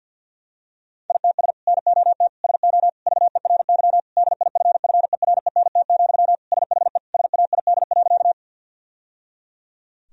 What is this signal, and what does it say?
Morse code, heard as a keyed tone, about 50 wpm, 700 Hz. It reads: ITS NOT SO VERY DIFFERENT, SHE SAID.
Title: Sentences from Top 1000 Words - Review Part 3 - 50wpm